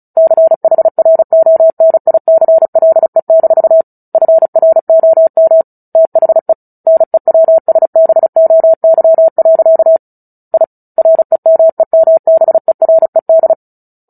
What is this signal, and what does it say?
Morse code, heard as a keyed tone, about 34 wpm, 650 Hz. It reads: CHRONICLE- FROM THE NEWSBOY. I REMEMBERED
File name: Book Two / 09-Chapter 9 - 35 WPM